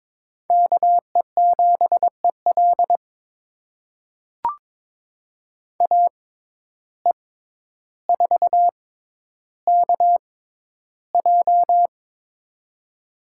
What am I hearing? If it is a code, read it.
Morse code, heard as a keyed tone, about 22 wpm, spaced as for 11 wpm, 700 Hz. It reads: KE7EL AE4KJ
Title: USA Call Signs - 5 Characters - 6x Character Spacing - 22wpm